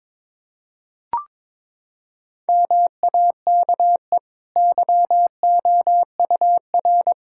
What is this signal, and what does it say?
Morse code, heard as a keyed tone, about 22 wpm, 700 Hz. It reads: MAKE YOUR
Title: Sentences from Top 1000 Words - Review Part 6 - 22wpm